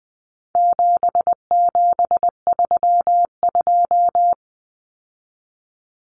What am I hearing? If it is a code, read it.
Morse code, heard as a keyed tone, about 20 wpm, 700 Hz. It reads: 7732
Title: Common Cut Numbers - 4-Digit 20wpm